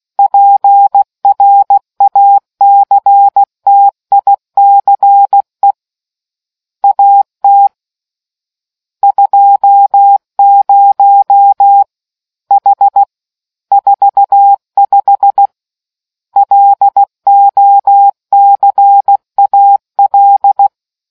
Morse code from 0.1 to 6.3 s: PRACTICE

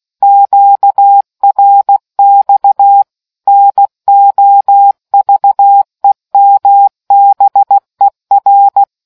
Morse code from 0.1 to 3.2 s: QRX

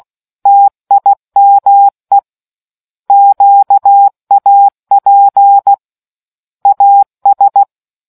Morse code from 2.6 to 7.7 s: QAP AS

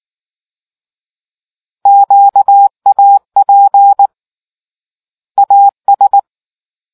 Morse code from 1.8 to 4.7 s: QAP

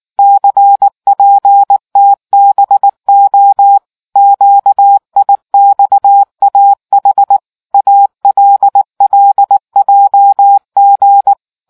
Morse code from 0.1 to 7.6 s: CPTBO QIXAH

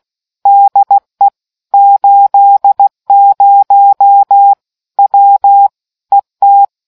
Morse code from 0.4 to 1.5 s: DE

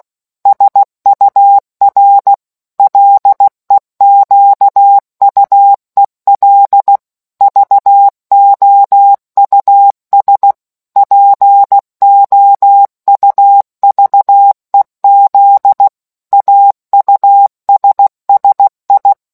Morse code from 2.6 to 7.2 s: LEQUEL

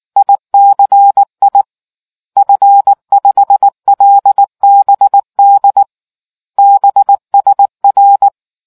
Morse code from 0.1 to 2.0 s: ICI

